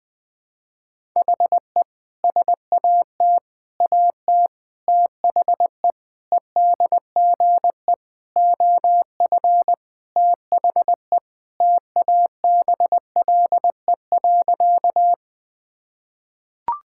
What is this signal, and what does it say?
Morse code, heard as a keyed tone, about 20 wpm, 700 Hz. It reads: HE SAT AT THE EDGE OF THE TABLE.
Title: Sentences with 3-Word Phrases - Spoken Beginning - 20wpm